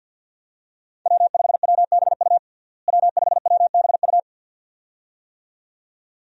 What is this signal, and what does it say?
Morse code, heard as a keyed tone, about 51 wpm, 700 Hz. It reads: W5WBU W5WBU
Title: USA Call Signs - 5 Characters - Repeated 2x - 50wpm